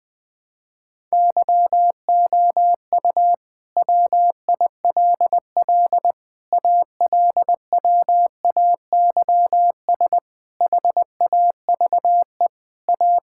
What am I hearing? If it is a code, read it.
Morse code, heard as a keyed tone, about 20 wpm, 700 Hz. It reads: YOU WILL ALWAYS HAVE A
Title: Sentences with 9-Word Phrases - Spoken Beginning - 20wpm